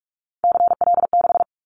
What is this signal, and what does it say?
Morse code, heard as a keyed tone, about 44 wpm, 700 Hz. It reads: CL6
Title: Callsign Prefixes for Top 100 DXCC - 45wpm